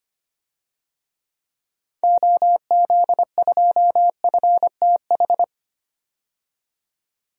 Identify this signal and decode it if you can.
Morse code, heard as a keyed tone, about 25 wpm, 700 Hz. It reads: OZ2FTH
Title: DXCC-100 Call Signs - 6 Characters - 25wpm